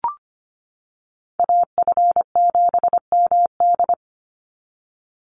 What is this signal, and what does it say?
Morse code, heard as a keyed tone, about 25 wpm, 700 Hz. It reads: AF7MD
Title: NA SOTA Chaser Callsigns 25wpm